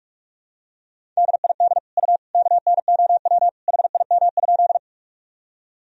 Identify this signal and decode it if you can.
Morse code, heard as a keyed tone, about 42 wpm, 700 Hz. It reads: DID U KNOW HIM?
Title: Sentences from Top 100 Words and Abbreviations 45wpm